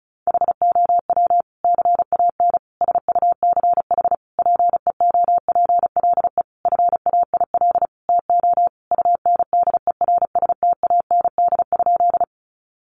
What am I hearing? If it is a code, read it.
Morse code, heard as a keyed tone, about 34 wpm, 700 Hz. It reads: HOW CAN SUCH PEOPLE FAIL TO UNDERSTAND?
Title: Sentences from Top 1100 Words 35wpm